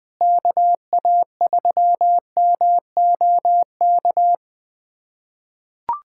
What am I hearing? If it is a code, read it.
Morse code, heard as a keyed tone, about 20 wpm, 700 Hz. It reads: KA3MOK
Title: USA Callsign DE Callsign - 6 Characters - 20wpm